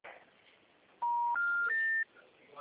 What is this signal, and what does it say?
Special information tone.